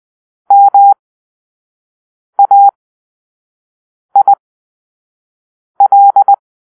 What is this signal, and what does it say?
Morse code, heard as a keyed tone, about 20 wpm, spaced as for 5 wpm, 800 Hz. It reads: MAIL
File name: Grp-3 Drill-1